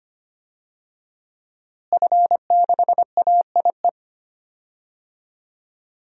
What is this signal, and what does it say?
Morse code, heard as a keyed tone, about 25 wpm, 700 Hz. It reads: F6AIE
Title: DXCC-100 Call Signs - 5 Characters - Repeated 3x - 25wpm